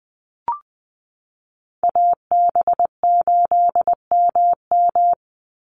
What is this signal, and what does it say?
Morse code, heard as a keyed tone, about 20 wpm, 700 Hz. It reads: AB8MM
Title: USA Call Signs 20wpm